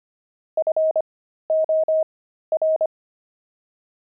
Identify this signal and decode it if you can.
Morse code, heard as a keyed tone, about 25 wpm, spaced as for 13 wpm, 630 Hz. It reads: FOR
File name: pangram-fs-13-cs-25-buckles